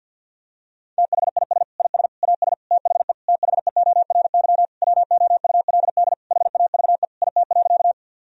Morse code, heard as a keyed tone, about 50 wpm, 700 Hz. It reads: THIS IS AS THE THEORY WOULD HAVE IT.
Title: Sentences from Top 1000 Words - Review Part 3 - 50wpm